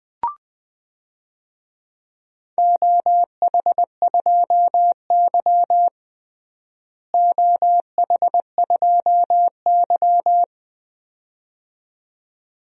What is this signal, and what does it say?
Morse code, heard as a keyed tone, about 20 wpm, 700 Hz. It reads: OH2Y OH2Y
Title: DXCC-100 Call Signs - 4 Characters - Repeated 2x - 20wpm